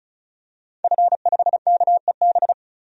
Morse code, heard as a keyed tone, about 36 wpm, 700 Hz. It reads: F5KEB
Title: DXCC-100 Call Signs - 5 Characters - 35wpm